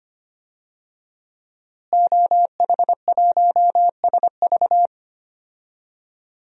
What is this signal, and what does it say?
Morse code, heard as a keyed tone, about 25 wpm, 700 Hz. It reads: OH1SV
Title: DXCC-100 Call Signs - 5 Characters - Repeated 2x - 25wpm